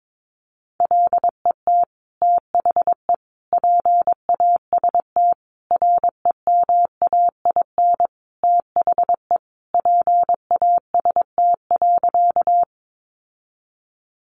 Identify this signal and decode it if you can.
Morse code, heard as a keyed tone, about 22 wpm, 700 Hz. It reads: LET THE PAST REMAIN THE PAST.
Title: Sentences from Top 1200 Words 22wpm